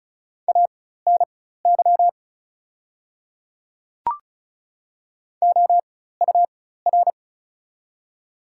Morse code, heard as a keyed tone, about 34 wpm, spaced as for 16 wpm, 700 Hz. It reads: ANY OUR
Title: 4x Character Spacing - Top 200 Words - 35wpm